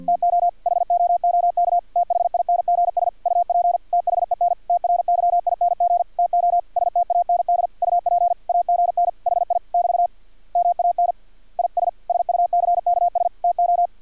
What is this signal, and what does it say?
Morse code, heard as a keyed tone, about 50 wpm, 700 Hz. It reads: TO HOOK THINGS UP THEN TRYING TO STAND UP AGN HI <BT> MAN IS SUCKS TO
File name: archived / CRSnet